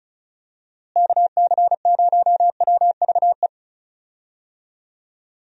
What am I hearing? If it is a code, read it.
Morse code, heard as a keyed tone, about 36 wpm, 700 Hz. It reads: KC0WVE
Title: USA Call Signs - 6 Characters - Repeated 2x - 35wpm